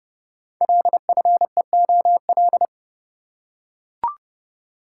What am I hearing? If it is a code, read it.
Morse code, heard as a keyed tone, about 30 wpm, 700 Hz. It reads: LFEOL
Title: Code Groups - 5 Letters - 30wpm